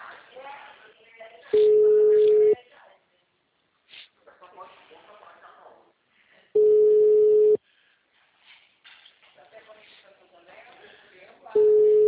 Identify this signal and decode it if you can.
Ringback tone (Europe, 425 Hz), with speech.
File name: out- -2014- - - .9201